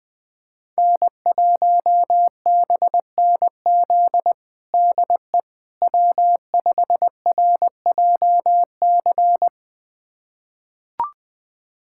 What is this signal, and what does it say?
Morse code, heard as a keyed tone, about 21 wpm, 700 Hz. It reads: N1BNZ DE W5RJC
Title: USA Callsign DE Callsign - 5 Characters - 20wpm